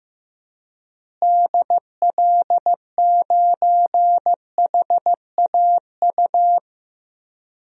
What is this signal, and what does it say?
Morse code, heard as a keyed tone, about 16 wpm, 700 Hz. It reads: DL9HAU
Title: DXCC-100 Call Signs - 6 Characters - Repeated 3x - 15wpm